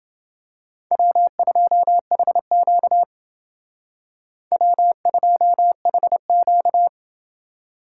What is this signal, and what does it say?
Morse code, heard as a keyed tone, about 28 wpm, 700 Hz. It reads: W2HQ W2HQ